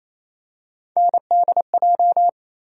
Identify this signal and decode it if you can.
Morse code, heard as a keyed tone, about 28 wpm, 700 Hz. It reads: NDJ
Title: USA Call Sign Postfixes - 3 Letters 28wpm